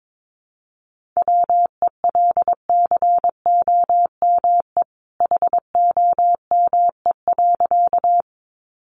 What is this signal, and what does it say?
Morse code, heard as a keyed tone, about 22 wpm, 700 Hz. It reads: WELCOME HOME.
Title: A Sound Of Thunder 22wpm